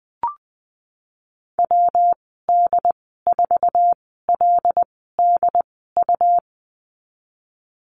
Morse code, heard as a keyed tone, about 20 wpm, spaced as for 15 wpm, 700 Hz. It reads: WD4LDU
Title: 128 Lesson 21g - Number 6 - Call Signs 20wpm